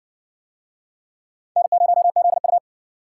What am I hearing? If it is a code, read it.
Morse code, heard as a keyed tone, about 60 wpm, 700 Hz. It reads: N0ZU